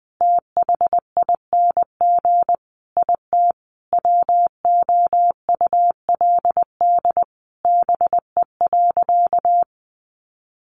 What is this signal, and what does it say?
Morse code, heard as a keyed tone, about 20 wpm, 700 Hz. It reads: THING IT WOULD BE.
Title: Sentences from Top 200 Words 20wpm